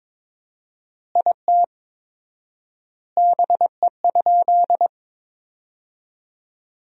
Morse code, heard as a keyed tone, about 21 wpm, 700 Hz. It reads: IT BE?
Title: 4x Word-Spacing - Sentences from Top 100 Words - 22wpm